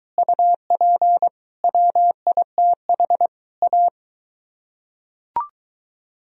Morse code, heard as a keyed tone, about 22 wpm, 700 Hz. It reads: UP WITH A